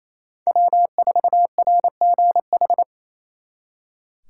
Morse code, heard as a keyed tone, about 35 wpm, 700 Hz. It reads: W4RGH
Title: Speed Racing - USA 1x3 Call Signs 28wpm